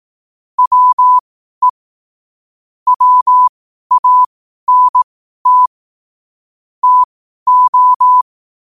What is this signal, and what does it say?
Morse code, heard as a keyed tone, about 18 wpm, spaced as for 12 wpm, 1000 Hz. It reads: WE WANT TO
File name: qotdfiles / 12wpm